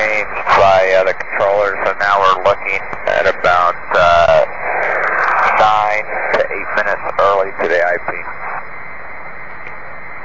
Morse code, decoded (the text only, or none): none